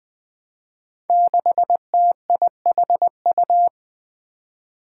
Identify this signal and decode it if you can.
Morse code, heard as a keyed tone, about 21 wpm, 700 Hz. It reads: 6TIHU
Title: Five Challenging Letter-Number Combinations 20wpm